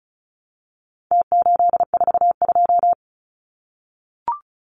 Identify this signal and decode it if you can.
Morse code, heard as a keyed tone, about 34 wpm, 700 Hz. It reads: T842
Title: Numbers - Contest Style - 4-Digit 35wpm